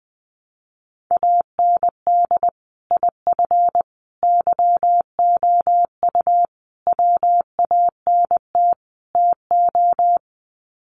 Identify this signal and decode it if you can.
Morse code, heard as a keyed tone, about 20 wpm, 700 Hz. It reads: AND IF YOU WANT TO
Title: Sets of 5 Words Encore 20wpm